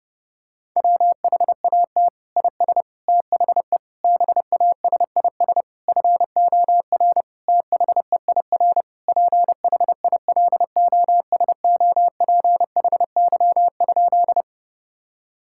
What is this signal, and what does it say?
Morse code, heard as a keyed tone, about 32 wpm, 700 Hz. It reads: WHAT IS THE BASIS FOR THEIR PHILOSOPHY?
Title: Sentences from Top 1000 Words - Review Part 6 - 30wpm